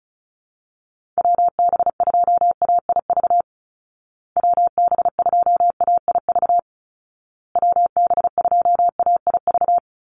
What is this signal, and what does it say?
Morse code, heard as a keyed tone, about 34 wpm, 700 Hz. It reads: WB2AIV WB2AIV WB2AIV
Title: Call Signs Repeated 2x - 35wpm